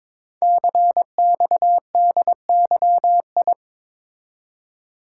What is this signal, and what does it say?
Morse code, heard as a keyed tone, about 23 wpm, 700 Hz. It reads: CXDYI
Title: Code Groups - 5 Letters - 22wpm